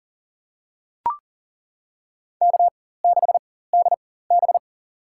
Morse code, heard as a keyed tone, about 38 wpm, spaced as for 18 wpm, 700 Hz. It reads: K6DB
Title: USA Call Signs - 4 Characters - 4x Character Spacing - 40wpm